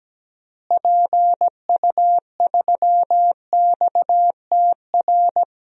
Morse code, heard as a keyed tone, about 17 wpm, 700 Hz. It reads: PU3XTR
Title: DXCC-100 Call Signs - 6 Characters - Repeated 2x - 17wpm